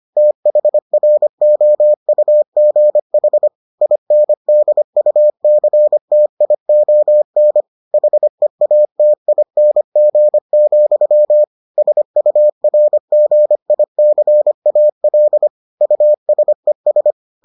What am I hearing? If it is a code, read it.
Morse code, heard as a keyed tone, about 25 wpm, 600 Hz. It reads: THROUGH INDUCTION HEATING, SURGICAL USES